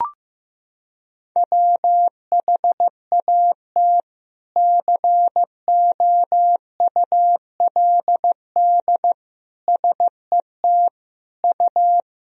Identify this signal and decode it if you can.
Morse code, heard as a keyed tone, about 15 wpm, 700 Hz. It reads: WHAT COULD SET U
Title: Sentences from Top 200 Words and Abbreviations 15wpm